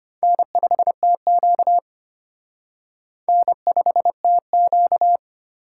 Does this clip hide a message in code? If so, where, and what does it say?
Yes: Morse code from 0.2 to 5.2 s, heard as a keyed tone, about 33 wpm, 700 Hz. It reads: N5TQ N5TQ